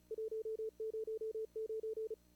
Morse code, heard as a keyed tone, about 29 wpm, 440 Hz. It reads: 109
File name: CWSprint / Feb2022 / k5tr-feb-na-sprint-cw-k5ot -0000